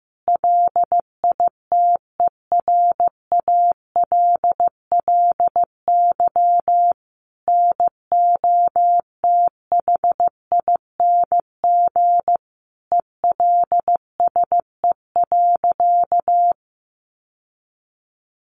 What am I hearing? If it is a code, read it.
Morse code, heard as a keyed tone, about 15 wpm, 700 Hz. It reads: LITERALLY NOTHING ELSE.
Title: Sentences from Top 1400 Words 15wpm